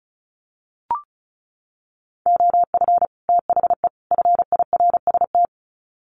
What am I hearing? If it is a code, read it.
Morse code, heard as a keyed tone, about 34 wpm, 700 Hz. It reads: OF THE FIRST
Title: Sets of 3 Words 35wpm